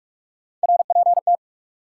Morse code, heard as a keyed tone, about 44 wpm, 700 Hz. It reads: RPT
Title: Top 180 QSO Elements 45wpm